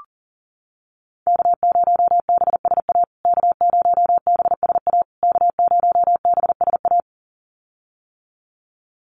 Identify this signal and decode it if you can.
Morse code, heard as a keyed tone, about 40 wpm, 700 Hz. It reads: K0BSA K0BSA K0BSA
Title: USA Call Signs Repeated 3x 40wpm